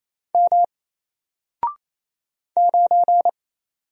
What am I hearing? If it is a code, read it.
Morse code, heard as a keyed tone, about 28 wpm, 700 Hz. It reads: M 9
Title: Single Character (Letter:Number:Punctuation:Prosign) Alt 28wpm